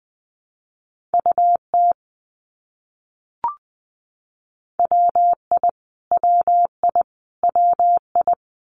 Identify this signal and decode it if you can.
Morse code, heard as a keyed tone, about 20 wpm, 700 Hz. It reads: UT WI WI WI
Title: US States Repeated 3x 20wpm